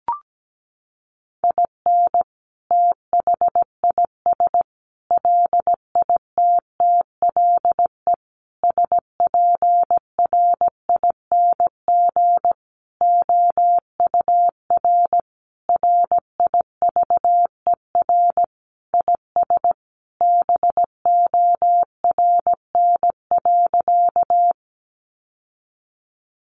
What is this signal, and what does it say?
Morse code, heard as a keyed tone, about 17 wpm, 700 Hz. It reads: IN THIS LITTLE SPRING OUR RIVER IS BORN.
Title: Sentences from Top 1200 Words 17wpm